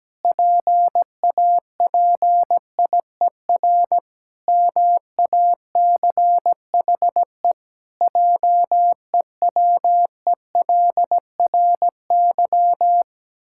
Morse code, heard as a keyed tone, about 17 wpm, 700 Hz. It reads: PAPIER MACHE JEWELRY
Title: Pangram Challenge v2 - 17wpm